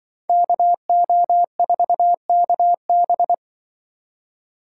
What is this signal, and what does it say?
Morse code, heard as a keyed tone, about 25 wpm, 700 Hz. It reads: KO4KB
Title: Speed Racing - USA 2x2 Call Signs 20wpm